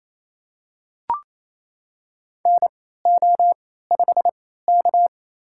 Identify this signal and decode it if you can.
Morse code, heard as a keyed tone, about 28 wpm, spaced as for 16 wpm, 700 Hz. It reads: NO5K
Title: USA Call Signs - 4 Characters - 3x Character Spacing - 28wpm